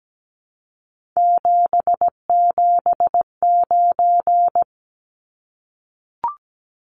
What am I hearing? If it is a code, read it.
Morse code, heard as a keyed tone, about 17 wpm, 700 Hz. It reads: 779
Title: Numbers - 3-Digit 17wpm